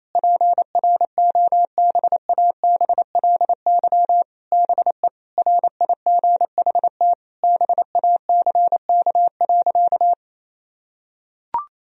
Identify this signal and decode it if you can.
Morse code, heard as a keyed tone, about 28 wpm, 700 Hz. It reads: PROBABLY BE RIGHT BACK.
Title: Sentences from Top 1000 Words - Review Part 4 - 28wpm